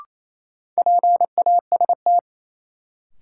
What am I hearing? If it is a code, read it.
Morse code, heard as a keyed tone, about 28 wpm, 700 Hz. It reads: PAST
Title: Sentences from Top 600 Words - New Word Drill - 28wpm